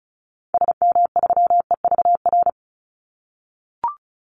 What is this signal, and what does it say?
Morse code, heard as a keyed tone, about 34 wpm, 700 Hz. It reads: SM3EVR
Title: Call Signs - 35wpm